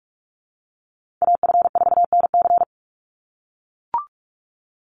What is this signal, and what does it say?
Morse code, heard as a keyed tone, about 44 wpm, 700 Hz. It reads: AF4NC
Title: Call Signs - 45wpm